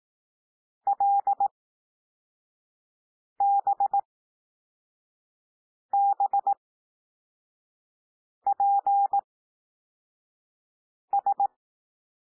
Morse code, heard as a keyed tone, about 18 wpm, spaced as for 4 wpm, 800 Hz. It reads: LBBPS
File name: cwmp3 / L2 SOUND LETTERS 18 wpm